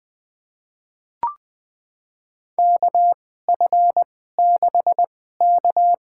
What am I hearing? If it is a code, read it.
Morse code, heard as a keyed tone, about 20 wpm, spaced as for 14 wpm, 700 Hz. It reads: KF6K
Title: USA Call Signs - 2x Character Spacing - 20wpm